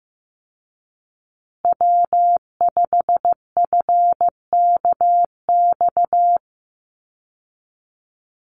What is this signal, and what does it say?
Morse code, heard as a keyed tone, about 15 wpm, 700 Hz. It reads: W5FKX
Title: Call Signs Repeated 3x - 15wpm